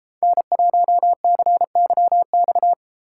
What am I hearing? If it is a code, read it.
Morse code, heard as a keyed tone, about 34 wpm, 700 Hz. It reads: N1CYX
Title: Speed Racing - USA 1x3 Call Signs 28wpm